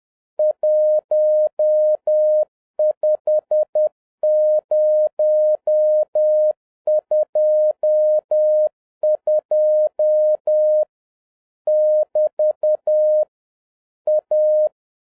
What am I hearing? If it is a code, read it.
Morse code, heard as a keyed tone, about 10 wpm, 610 Hz. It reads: 15022 <BT> A